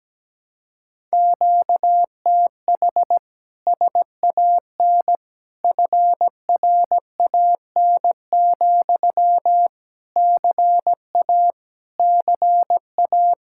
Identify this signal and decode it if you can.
Morse code, heard as a keyed tone, about 17 wpm, 700 Hz. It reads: QTH SAN FRAN, CA CA
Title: Speed-Racing Common QSO Phrases Encore - 17wpm